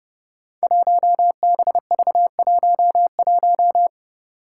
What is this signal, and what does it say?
Morse code, heard as a keyed tone, about 30 wpm, 700 Hz. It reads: 1BV11
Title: Code Groups - 5 Letter:Numbers - 30wpm